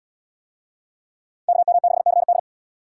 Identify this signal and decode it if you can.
Morse code, heard as a keyed tone, about 70 wpm, 700 Hz. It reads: KM6CD